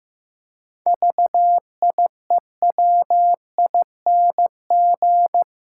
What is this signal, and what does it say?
Morse code, heard as a keyed tone, about 15 wpm, 700 Hz. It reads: VIEWING
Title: Pangram Challenge v2 - 15wpm